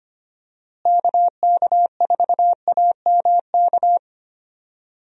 Morse code, heard as a keyed tone, about 25 wpm, 700 Hz. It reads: KK4AMK
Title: USA Call Signs - 6 Characters - Repeated 3x - 25wpm